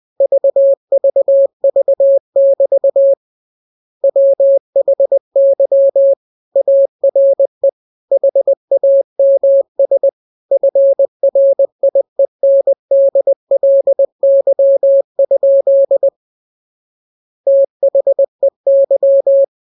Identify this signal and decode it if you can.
Morse code, heard as a keyed tone, about 20 wpm, 550 Hz. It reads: VVV<BT> WHY ARE HAMS FRIENDLY? THEY